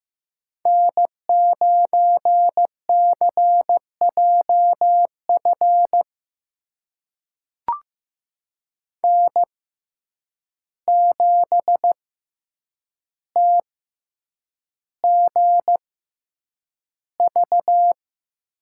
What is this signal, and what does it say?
Morse code, heard as a keyed tone, about 15 wpm, spaced as for 8 wpm, 700 Hz. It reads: N9CJF N7TGV
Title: USA Call Signs - 5 Characters - 6x Character Spacing - 15wpm